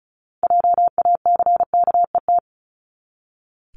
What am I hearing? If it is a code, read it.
Morse code, heard as a keyed tone, about 34 wpm, 700 Hz. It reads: JACKET
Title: Top 1751-2000 Words 35wpm